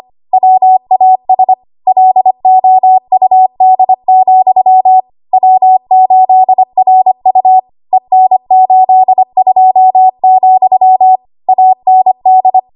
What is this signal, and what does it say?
Morse code, heard as a keyed tone, about 25 wpm, 750 Hz. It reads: WAS LOUD, W8RU EN82, AND
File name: Morse / 25WPM